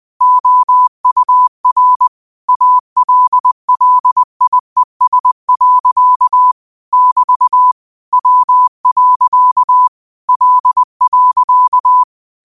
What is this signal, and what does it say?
Morse code, heard as a keyed tone, about 20 wpm, 1000 Hz. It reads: OUR ALLIES. <BT> W. L.